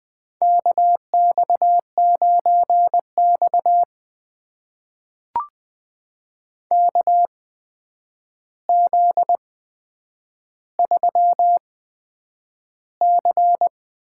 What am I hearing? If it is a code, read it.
Morse code, heard as a keyed tone, about 20 wpm, spaced as for 9 wpm, 700 Hz. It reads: KX9X KZ3C
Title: USA Call Signs - 4 Characters - 8x Character Spacing - 20wpm